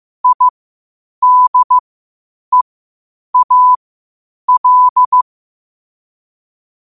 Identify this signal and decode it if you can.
Morse code, heard as a keyed tone, about 15 wpm, spaced as for 8 wpm, 1000 Hz. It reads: IDEAL
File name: qotdfiles / 8wpm